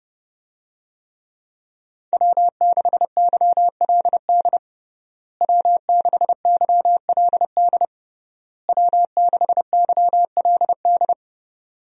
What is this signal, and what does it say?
Morse code, heard as a keyed tone, about 30 wpm, 700 Hz. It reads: W6YLD W6YLD W6YLD
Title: USA Call Signs - 5 Characters - Repeated 3x - 30wpm